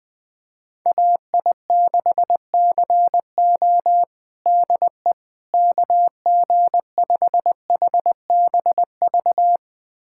Morse code, heard as a keyed tone, about 20 wpm, 700 Hz. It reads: AI6CO DE KG5HBV
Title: USA Callsign DE Callsign 20wpm